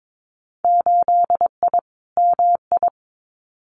Morse code, heard as a keyed tone, about 22 wpm, 700 Hz. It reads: 8I MI
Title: Winter Field Day Exchanges Encore 22wpm